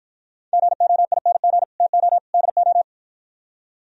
Morse code, heard as a keyed tone, about 52 wpm, 700 Hz. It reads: GOING TO DO